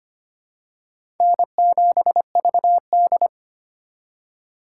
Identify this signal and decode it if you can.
Morse code, heard as a keyed tone, about 25 wpm, 700 Hz. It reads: N7VD
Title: USA Call Signs - 4 Characters - 4x Character Spacing - 25wpm